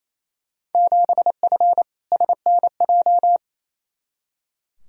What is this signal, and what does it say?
Morse code, heard as a keyed tone, about 27 wpm, 700 Hz. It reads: 7F SNJ
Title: Field Day Exchanges Encore - 28wpm